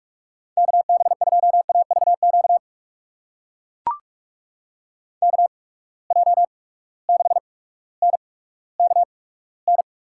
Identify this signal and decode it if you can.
Morse code, heard as a keyed tone, about 42 wpm, spaced as for 19 wpm, 700 Hz. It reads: KB2AVQ KJ6NKN
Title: USA Call Signs - 6 Characters - 8x Character Spacing - 45wpm